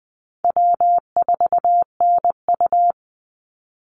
Morse code, heard as a keyed tone, about 20 wpm, 700 Hz. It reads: W4NU
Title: Easy Call Signs Repeated 3x - 20wpm